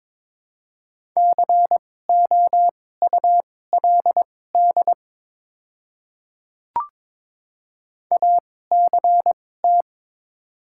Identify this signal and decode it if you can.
Morse code, heard as a keyed tone, about 22 wpm, spaced as for 16 wpm, 700 Hz. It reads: COULD ACT
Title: 2x Character Spacing - Top 300 Words - 22wpm